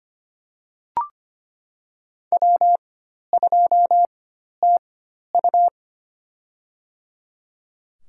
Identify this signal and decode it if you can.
Morse code, heard as a keyed tone, about 25 wpm, spaced as for 12 wpm, 700 Hz. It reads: W2TU
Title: USA Call Signs - 4 Characters - 4x Character Spacing - 25wpm